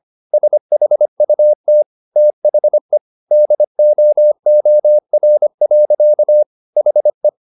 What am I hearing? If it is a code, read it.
Morse code, heard as a keyed tone, about 25 wpm, 600 Hz. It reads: SHUT THE DOOR. HE